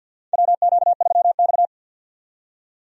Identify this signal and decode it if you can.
Morse code, heard as a keyed tone, about 53 wpm, 700 Hz. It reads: WQ3X
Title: USA Call Signs - 4 Characters - 50wpm